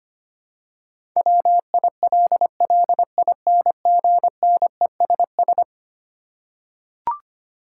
Morse code, heard as a keyed tone, about 25 wpm, 700 Hz. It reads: WILLINGNESS
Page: morse-code-ninja-practice-25wpm